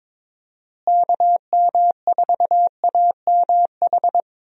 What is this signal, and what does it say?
Morse code, heard as a keyed tone, about 21 wpm, 700 Hz. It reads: KM4AMH
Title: USA Call Signs - 6 Characters - Repeated 3x - 22wpm